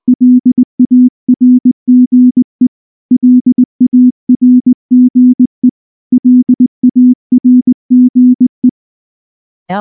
Morse code, heard as a keyed tone, about 20 wpm, 250 Hz. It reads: LARGE LARGE LARGE
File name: morse-training-20WPM